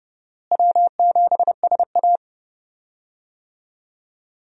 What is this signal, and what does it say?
Morse code, heard as a keyed tone, about 30 wpm, 700 Hz. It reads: W7SA
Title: USA Call Signs - 4 Characters - Repeated 2x - 30wpm